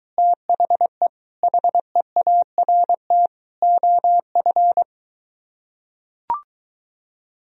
Morse code, heard as a keyed tone, about 23 wpm, 700 Hz. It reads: THE HEART OF